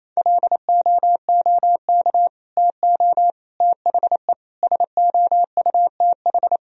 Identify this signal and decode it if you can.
Morse code, heard as a keyed tone, about 28 wpm, 700 Hz. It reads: LOOK TO THE SOUTH
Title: Sentences from Top 1000 Words - Review Part 3 - 28wpm